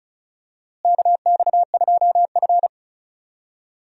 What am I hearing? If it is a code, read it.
Morse code, heard as a keyed tone, about 36 wpm, 700 Hz. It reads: KX2F
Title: USA Call Signs - 4 Characters - Repeated 2x - 35wpm